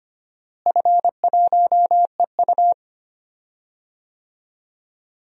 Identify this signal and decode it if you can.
Morse code, heard as a keyed tone, about 25 wpm, 700 Hz. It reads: F1EU
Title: DXCC-100 Call Signs - 4 Characters - Repeated 2x - 25wpm